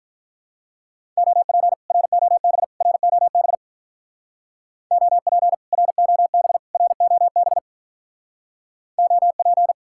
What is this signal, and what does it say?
Morse code, heard as a keyed tone, about 46 wpm, 700 Hz. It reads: OP ROB ROB OP ROB ROB OP